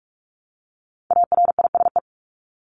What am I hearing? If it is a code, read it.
Morse code, heard as a keyed tone, about 44 wpm, 700 Hz. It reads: ARISE
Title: Top 1501-1750 Words 45wpm